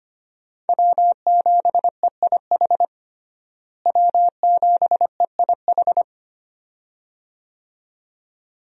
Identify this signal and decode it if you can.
Morse code, heard as a keyed tone, about 25 wpm, 700 Hz. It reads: W7EIH W7EIH
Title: USA Call Signs - 5 Characters - Repeated 2x - 25wpm